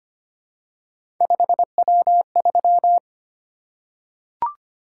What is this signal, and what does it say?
Morse code, heard as a keyed tone, about 25 wpm, 700 Hz. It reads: 5W3
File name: Three Letter-Number Combinations - 25wpm